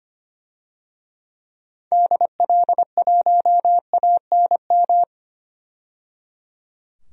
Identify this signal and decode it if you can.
Morse code, heard as a keyed tone, about 25 wpm, 700 Hz. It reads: DL1ANM
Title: DXCC-100 Call Signs - 6 Characters - 25wpm